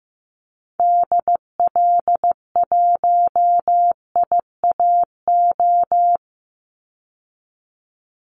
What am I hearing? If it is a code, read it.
Morse code, heard as a keyed tone, about 15 wpm, 700 Hz. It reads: DL1IAO
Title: Call Signs Repeated 2x - 15wpm